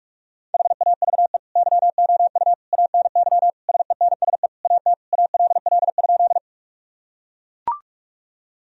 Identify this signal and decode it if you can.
Morse code, heard as a keyed tone, about 41 wpm, 700 Hz. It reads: HAVE YOU ANY SENSE AT ALL?
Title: Sentences from Top 1000 Words - Review Part 4 - 45wpm